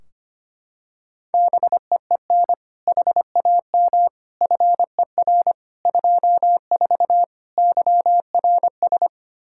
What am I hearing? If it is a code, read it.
Morse code, heard as a keyed tone, about 25 wpm, 700 Hz. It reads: BEEN HAM FER 24 YRS
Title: Practice QSO - Exchange 3 - More Info - 25wpm